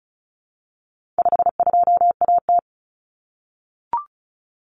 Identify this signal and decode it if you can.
Morse code, heard as a keyed tone, about 34 wpm, 700 Hz. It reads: 52AT
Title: Common Cut Numbers - 4-Digit 35wpm